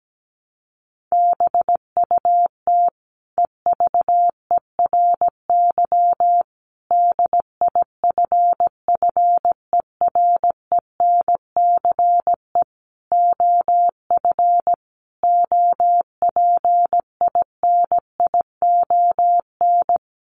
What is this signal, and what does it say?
Morse code, heard as a keyed tone, about 17 wpm, 700 Hz. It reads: BUT EVERY DIFFERENCE OF OPINION
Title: Sentences from Top 1400 Words 17wpm